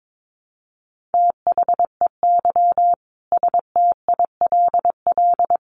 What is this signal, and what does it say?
Morse code, heard as a keyed tone, about 22 wpm, 700 Hz. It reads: THEY STILL
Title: Sentences from Top 1000 Words 22wpm